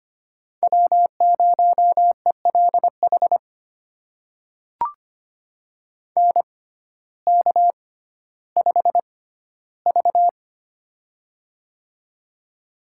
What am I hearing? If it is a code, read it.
Morse code, heard as a keyed tone, about 25 wpm, spaced as for 14 wpm, 700 Hz. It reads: W0ELH NK5V
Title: USA Call Signs - 6x Character Spacing - 25wpm